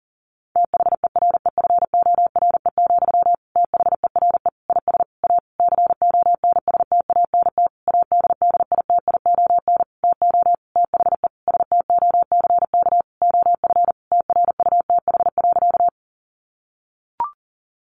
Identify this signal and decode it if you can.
Morse code, heard as a keyed tone, about 40 wpm, 700 Hz. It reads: THEREFORE, THERE IS A CONSTANT ADDITION TO THE STOCK OF TRUTH.
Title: Sentences from Top 1500 Words 40wpm